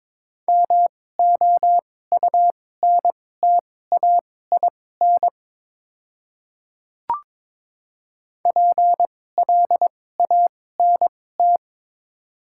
Morse code, heard as a keyed tone, about 21 wpm, spaced as for 16 wpm, 690 Hz. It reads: MOUNTAIN PLANT
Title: 2x Character Spacing - Top 300 Words - 22wpm